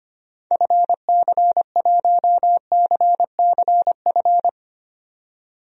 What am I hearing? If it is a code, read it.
Morse code, heard as a keyed tone, about 25 wpm, 700 Hz. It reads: FC1CCF
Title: DXCC-100 Call Signs - 6 Characters - 25wpm